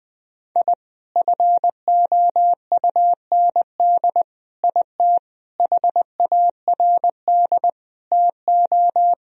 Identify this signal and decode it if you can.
Morse code, heard as a keyed tone, about 20 wpm, 700 Hz. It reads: I FOUND IT HARD TO
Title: Sentences from Top 1000 Words - Review Part 1 - 20wpm